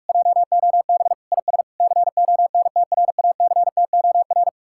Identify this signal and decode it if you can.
Morse code, heard as a keyed tone, about 58 wpm, 700 Hz. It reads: JOB IS CONTRACTOR